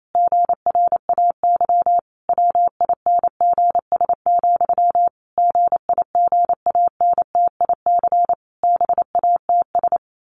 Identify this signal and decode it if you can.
Morse code, heard as a keyed tone, about 28 wpm, 700 Hz. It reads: GRAY WINGS, GIGANTIC BATS
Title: A Sound Of Thunder 28wpm